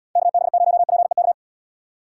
Morse code, heard as a keyed tone, about 77 wpm, 700 Hz. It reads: KB0PEK